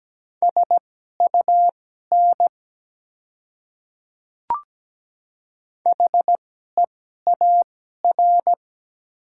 Morse code, heard as a keyed tone, about 17 wpm, spaced as for 12 wpm, 690 Hz. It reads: SUN HEAR
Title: 2x Character Spacing - Top 300 Words - 17wpm